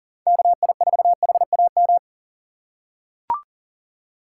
Morse code, heard as a keyed tone, about 40 wpm, 700 Hz. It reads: KI4HAM
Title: USA Call Signs - 6 Characters - 4x Character Spacing - 40wpm